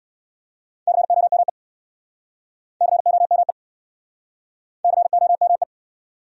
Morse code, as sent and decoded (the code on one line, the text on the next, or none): -.-. --- -- . / -.-. --- -- . / -.-. --- -- .
COME COME COME